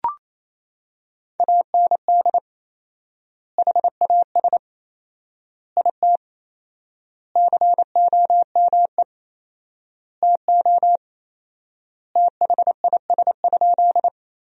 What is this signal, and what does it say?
Morse code, heard as a keyed tone, about 28 wpm, 700 Hz. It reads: AND HAS IT COME TO THIS?
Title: 4x Word-Spacing - Sentences from Top 100 Words - 28wpm